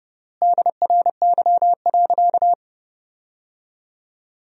Morse code, heard as a keyed tone, about 30 wpm, 700 Hz. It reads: DRY.
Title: 3x Word-Spacing - Sentences from Top 500 Words - 30wpm